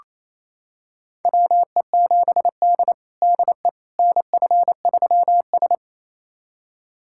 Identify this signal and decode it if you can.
Morse code, heard as a keyed tone, about 28 wpm, 700 Hz. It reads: WE7D DE NF3S
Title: USA Callsign DE Callsign - 4 Characters - 28wpm